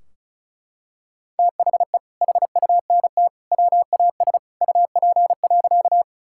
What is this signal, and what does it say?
Morse code, heard as a keyed tone, about 36 wpm, 700 Hz. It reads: THE HUNT WAS UP.
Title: Sentences from Top 1600 Words 35wpm